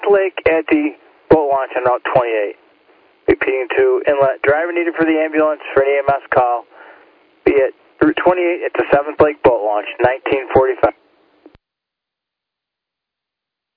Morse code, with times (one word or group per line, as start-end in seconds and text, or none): none